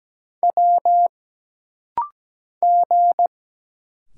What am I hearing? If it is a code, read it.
Morse code, heard as a keyed tone, about 17 wpm, 700 Hz. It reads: W G
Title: Single Letters - Rapid-Fire 17wpm